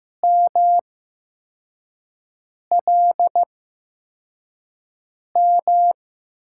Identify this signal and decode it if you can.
Morse code, heard as a keyed tone, about 16 wpm, spaced as for 4 wpm, 700 Hz. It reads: MLM
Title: USA Call Signs - 5 Characters - 8x Character Spacing - 15wpm